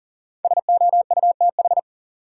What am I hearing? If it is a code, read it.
Morse code, heard as a keyed tone, about 40 wpm, 700 Hz. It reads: SOUTH
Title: Sentences from Top 400 Words - New Word Drill - 40wpm